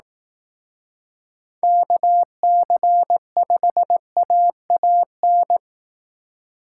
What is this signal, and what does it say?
Morse code, heard as a keyed tone, about 18 wpm, 700 Hz. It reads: KC5AAN
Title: Speed Racing - USA 2x3 Call Signs 15wpm